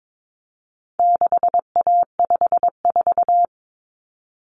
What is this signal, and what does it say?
Morse code, heard as a keyed tone, about 22 wpm, 700 Hz. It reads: 6A54
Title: Common Cut Numbers - 4-Digit 22wpm